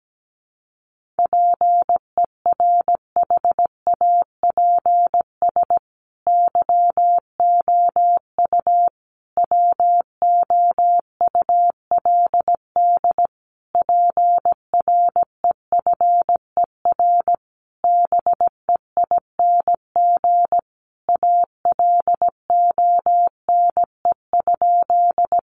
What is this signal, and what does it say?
Morse code, heard as a keyed tone, about 17 wpm, 700 Hz. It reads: PERHAPS YOU WOULD PREFER BEING ALONE?
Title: Sentences from Top 1100 Words 17wpm